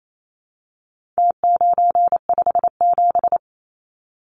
Morse code, heard as a keyed tone, about 28 wpm, 700 Hz. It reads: T957
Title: Numbers - Contest Style - 4-Digit 28wpm